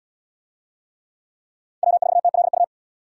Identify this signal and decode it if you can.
Morse code, heard as a keyed tone, about 77 wpm, 700 Hz. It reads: W4TLU